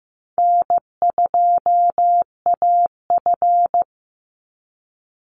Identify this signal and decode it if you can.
Morse code, heard as a keyed tone, about 15 wpm, 700 Hz. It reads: N2AF
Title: NA SOTA Chaser Callsigns 15wpm